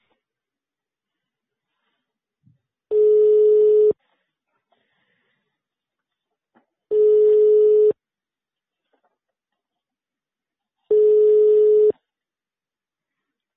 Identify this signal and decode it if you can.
Ringback tone (425 Hz, 1 s on, 3 s off).